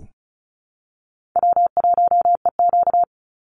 Morse code, heard as a keyed tone, about 36 wpm, 700 Hz. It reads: W1EQ
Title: Call Signs - 35wpm